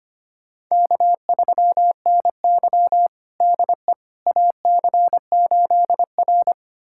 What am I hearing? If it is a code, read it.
Morse code, heard as a keyed tone, about 25 wpm, 700 Hz. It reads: K3NY DE AC8R
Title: USA Callsign DE Callsign - 4 Characters - 25wpm